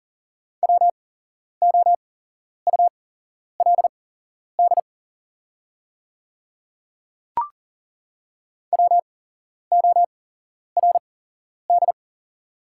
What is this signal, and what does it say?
Morse code, heard as a keyed tone, about 39 wpm, spaced as for 11 wpm, 700 Hz. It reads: WOULD WORD
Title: 8x Character Spacing - Top 100 Words - 40wpm